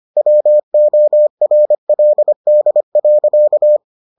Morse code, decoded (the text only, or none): WORLD.